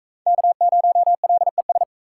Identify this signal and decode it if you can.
Morse code, heard as a keyed tone, about 44 wpm, 700 Hz. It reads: K0LES